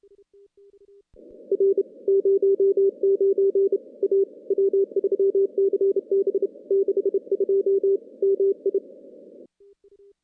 Morse code, heard as a keyed tone, about 29 wpm, 390 Hz. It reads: R 09 A W3CB 62 MI
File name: contests / CW-SS-k5tr- -2100